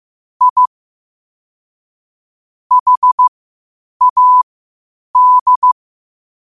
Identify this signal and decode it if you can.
Morse code, heard as a keyed tone, about 15 wpm, spaced as for 8 wpm, 1000 Hz. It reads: I HAD
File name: qotdfiles / 8wpm